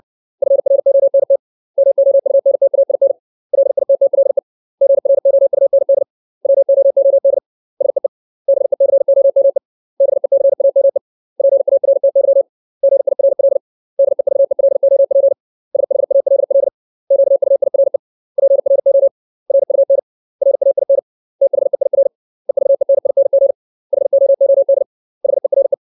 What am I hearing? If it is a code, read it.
Morse code, heard as a keyed tone, about 60 wpm, 550 Hz. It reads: FRONT MOUNTAIN LITTLE GROUND WOOD SEE BECOME BECAME PARTY GIRL DEVELOP SHALL QUIRE PRO RUN RAIN THEIR EVENING HOOD HERE
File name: 632 words / 632 Words 60WPM